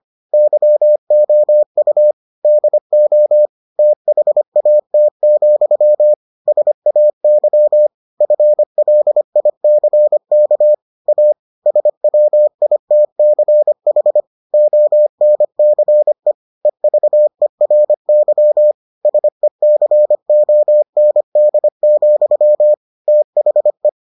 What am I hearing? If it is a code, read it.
Morse code, heard as a keyed tone, about 25 wpm, 600 Hz. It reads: YOU DO THAT, SAY FLICK A SWITCH ONCE EVERY SECOND, THE